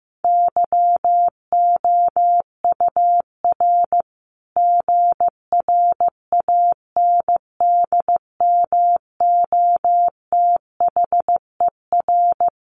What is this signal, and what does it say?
Morse code, heard as a keyed tone, about 15 wpm, 700 Hz. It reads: YOUR GRANDMOTHER
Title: Sentences from Top 1200 Words 15wpm